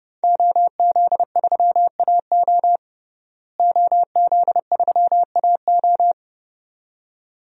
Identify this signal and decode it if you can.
Morse code, heard as a keyed tone, about 32 wpm, 700 Hz. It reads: OZ3AO OZ3AO
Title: DXCC-100 Call Signs - 5 Characters - Repeated 2x - 30wpm